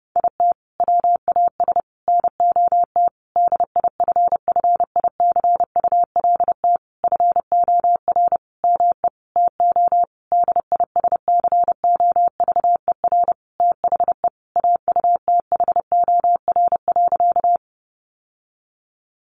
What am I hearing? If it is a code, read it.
Morse code, heard as a keyed tone, about 30 wpm, 700 Hz. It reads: IT WAS NOT DIFFICULT FOR ME TO DISCOVER THE AUTHOR.
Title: Sentences from Top 1200 Words 30wpm